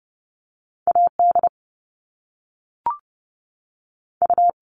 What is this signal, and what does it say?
Morse code, heard as a keyed tone, about 30 wpm, 700 Hz. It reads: A D U
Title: All Cut Numbers - 2-Digit 30wpm